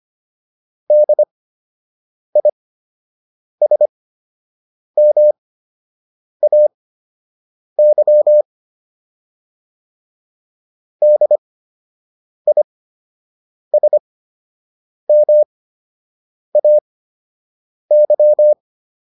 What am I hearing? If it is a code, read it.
Morse code, heard as a keyed tone, about 25 wpm, spaced as for 7 wpm, 600 Hz. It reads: DISMAY DISMAY